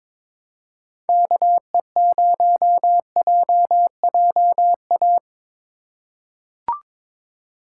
Morse code, heard as a keyed tone, about 22 wpm, 700 Hz. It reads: KE0JJA
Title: USA Call Signs - 6 Characters - 3x Character Spacing - 22wpm